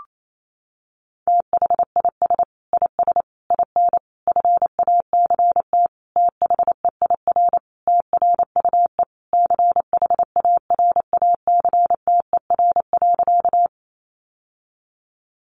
Morse code, heard as a keyed tone, about 28 wpm, 700 Hz. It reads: THIS IS IN FACT THEIR TRUE CHARACTER.
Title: Sentences from Top 1200 Words 28wpm